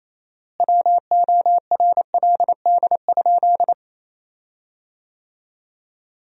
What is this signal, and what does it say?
Morse code, heard as a keyed tone, about 28 wpm, 700 Hz. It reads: WORLD?
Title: 4x Word-Spacing - Sentences from Top 200 Words - 28wpm